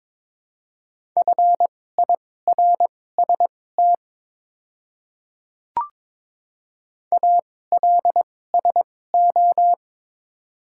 Morse code, heard as a keyed tone, about 22 wpm, spaced as for 16 wpm, 700 Hz. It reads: FIRST ALSO
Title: 2x Character Spacing - Top 200 Words - 22wpm